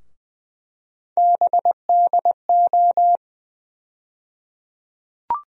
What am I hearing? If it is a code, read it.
Morse code, heard as a keyed tone, about 20 wpm, 700 Hz. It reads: BDO
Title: USA Call Sign Postfixes - 3 Letters 20wpm